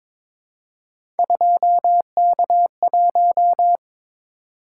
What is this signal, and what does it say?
Morse code, heard as a keyed tone, about 22 wpm, 700 Hz. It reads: 2K1
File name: practice / 22wpm / Three Letter-Number Combinations - 22wpm